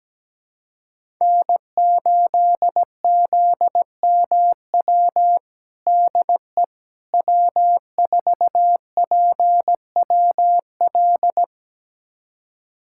Morse code, heard as a keyed tone, about 17 wpm, 700 Hz. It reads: N8ZMW DE W4PWL
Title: USA Callsign DE Callsign - 5 Characters - 17wpm